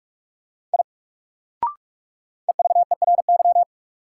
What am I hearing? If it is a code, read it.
Morse code, heard as a keyed tone, about 41 wpm, 700 Hz. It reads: I EVERY
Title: Sentences from Top 200 Words - New Word Drill - 45wpm